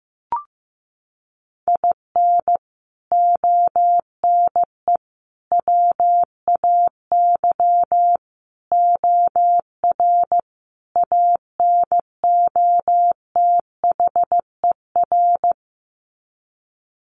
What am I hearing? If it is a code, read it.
Morse code, heard as a keyed tone, about 15 wpm, 700 Hz. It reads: IN ONE WAY OR ANOTHER
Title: Sets of 5 Words Encore 15wpm